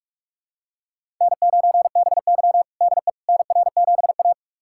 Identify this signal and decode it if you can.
Morse code, heard as a keyed tone, about 46 wpm, 700 Hz. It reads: N9BY DE NR7A
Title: USA Callsign DE Callsign - 4 Characters - 45wpm